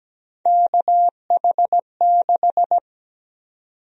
Morse code, heard as a keyed tone, about 17 wpm, 700 Hz. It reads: KH6
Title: USA Call Sign Prefixes - Repeated 2x - 17wpm